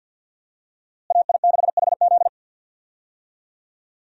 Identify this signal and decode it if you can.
Morse code, heard as a keyed tone, about 48 wpm, 700 Hz. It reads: AI6HZ
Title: USA Call Signs - 5 Characters - Repeated 3x - 50wpm